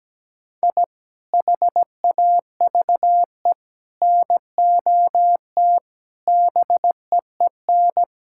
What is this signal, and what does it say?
Morse code, heard as a keyed tone, about 17 wpm, 700 Hz. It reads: I HAVE NOT BEEN
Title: Sentences from Top 1000 Words - Review Part 6 - 17wpm